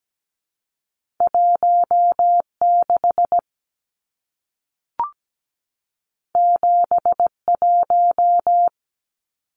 Morse code, heard as a keyed tone, about 17 wpm, 700 Hz. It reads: 16 71
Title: Numbers - 2-Digit 17wpm